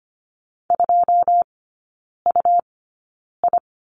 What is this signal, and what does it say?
Morse code, heard as a keyed tone, about 25 wpm, spaced as for 9 wpm, 700 Hz. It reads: 2UI
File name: Lesson 17e - Letter u - Three Characters 25wpm